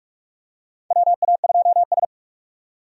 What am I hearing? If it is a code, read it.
Morse code, heard as a keyed tone, about 46 wpm, 700 Hz. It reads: WA2S
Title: USA Call Signs - 4 Characters - 45wpm